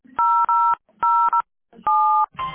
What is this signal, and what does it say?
DTMF, heard as telephone touch tones, tones of about 400 ms. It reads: 007